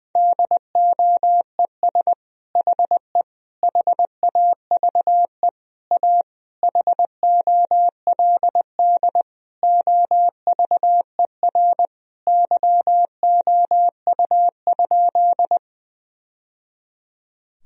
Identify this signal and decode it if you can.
Morse code, heard as a keyed tone, about 21 wpm, 700 Hz. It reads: DOES HE HAVE A HOLD OVER YOU?
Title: Sentences from Top 1000 Words - Review Part 2 - 20wpm